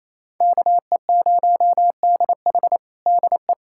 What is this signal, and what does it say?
Morse code, heard as a keyed tone, about 28 wpm, 700 Hz. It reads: KE0DH DE
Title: USA Callsign DE Callsign - 5 Characters - 28wpm